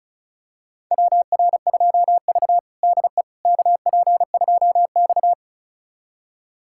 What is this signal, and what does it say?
Morse code, heard as a keyed tone, about 36 wpm, 700 Hz. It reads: WR2V DE KP2X
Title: USA Callsign DE Callsign - 4 Characters - 35wpm